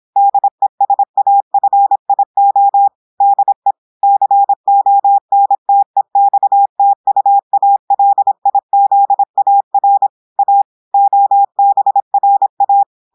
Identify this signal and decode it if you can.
Morse code, heard as a keyed tone, about 26 wpm, 800 Hz. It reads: DESAFIO DE CONTEXTUALIZAR A OBRA
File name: audios / portugais / portugais texte en clair 1 vitesse 26 mots minutes